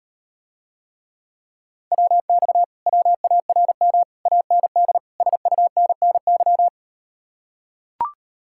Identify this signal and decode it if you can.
Morse code, heard as a keyed tone, about 39 wpm, 700 Hz. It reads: WX WARM AND SUNNY